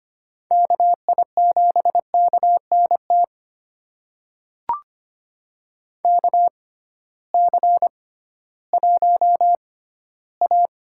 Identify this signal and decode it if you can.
Morse code, heard as a keyed tone, about 25 wpm, spaced as for 15 wpm, 700 Hz. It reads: KI7KNT KC1A
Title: USA Call Signs - 6 Characters - 6x Character Spacing - 25wpm